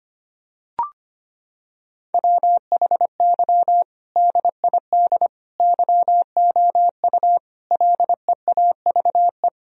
Morse code, heard as a keyed tone, about 25 wpm, 700 Hz. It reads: WHY DID YOU LEAVE
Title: Sentences from Top 1000 Words - Review Part 4 - 25wpm